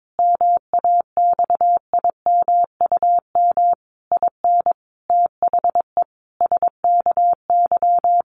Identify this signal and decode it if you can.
Morse code, heard as a keyed tone, about 22 wpm, 700 Hz. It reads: MAXIMUM IN THE SKY
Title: Sentences from Top 1300 Words 22wpm